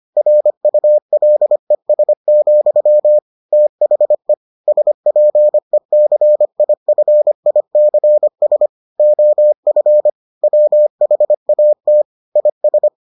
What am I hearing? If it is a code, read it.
Morse code, heard as a keyed tone, about 25 wpm, 600 Hz. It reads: RULES, THE SPECIFICS OF WHAT IS